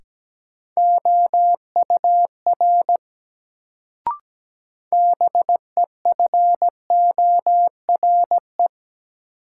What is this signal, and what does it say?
Morse code, heard as a keyed tone, about 17 wpm, 700 Hz. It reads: OUR BEFORE
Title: Sentences from Top 200 Words - New Word Drill - 17wpm